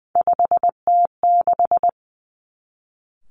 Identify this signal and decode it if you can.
Morse code, heard as a keyed tone, about 20 wpm, 700 Hz. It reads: 5T6
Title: Three Challenging Letter-Number Combinations 20wpm